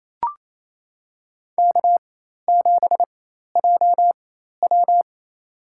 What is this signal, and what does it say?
Morse code, heard as a keyed tone, about 28 wpm, spaced as for 13 wpm, 700 Hz. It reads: K7JW
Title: USA Call Signs - 4 Characters - 4x Character Spacing - 28wpm